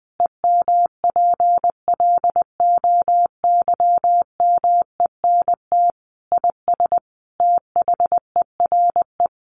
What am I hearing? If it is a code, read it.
Morse code, heard as a keyed tone, about 20 wpm, 700 Hz. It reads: EMPLOYMENT IS THERE
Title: Sentences from Top 1500 Words 20wpm